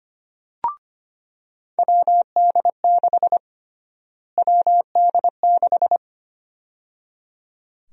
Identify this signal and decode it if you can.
Morse code, heard as a keyed tone, about 25 wpm, 700 Hz. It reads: WD6 WD6
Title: USA Call Sign Prefixes - Repeated 2x - 25wpm